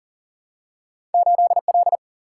Morse code, heard as a keyed tone, about 38 wpm, 700 Hz. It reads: 8L
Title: Single-Letter Single-Number 40wpm